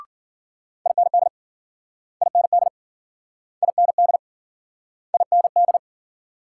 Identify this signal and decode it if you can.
Morse code, heard as a keyed tone, about 51 wpm, 700 Hz. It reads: IND IND IND IND